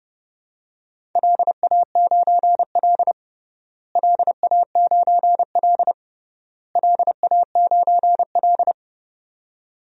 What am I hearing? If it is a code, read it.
Morse code, heard as a keyed tone, about 31 wpm, 700 Hz. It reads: LA9L LA9L LA9L
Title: DXCC-100 Call Signs - 4 Characters - Repeated 3x - 30wpm